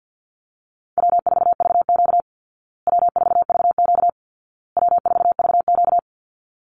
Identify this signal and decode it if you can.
Morse code, heard as a keyed tone, about 50 wpm, 700 Hz. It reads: W4VQ W4VQ W4VQ
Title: Call Signs Repeated 2x - 50wpm